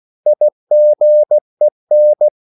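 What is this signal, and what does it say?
Morse code, heard as a keyed tone, about 16 wpm, 600 Hz. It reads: IGEN
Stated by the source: Slow (QRS) Morse code at 16 wpm, from CW med Gnister.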